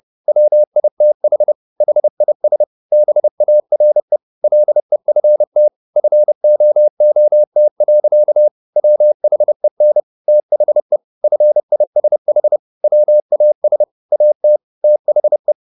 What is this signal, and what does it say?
Morse code, heard as a keyed tone, about 30 wpm, 600 Hz. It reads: WITH HIS BARE LEFT FOOT. WHEN THE FISH WAS AT THE